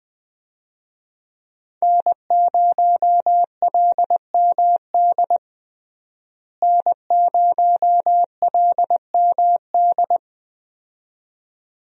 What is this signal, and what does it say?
Morse code, heard as a keyed tone, about 21 wpm, 700 Hz. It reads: N0LMD N0LMD
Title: USA Call Signs - 5 Characters - Repeated 2x - 20wpm